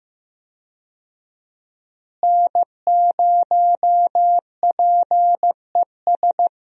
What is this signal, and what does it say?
Morse code, heard as a keyed tone, about 15 wpm, 700 Hz. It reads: N0PES
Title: USA Call Signs - 5 Characters - Repeated 3x - 15wpm